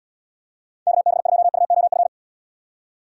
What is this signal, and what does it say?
Morse code, heard as a keyed tone, about 76 wpm, 700 Hz. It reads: KB2UOF